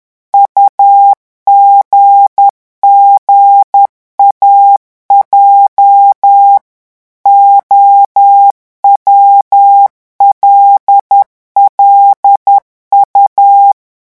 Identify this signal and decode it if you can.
Morse code, heard as a keyed tone, about 10 wpm, 800 Hz. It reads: UGGAJ OWLLU
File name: morse mp3 / revision8-vitesse 10 mots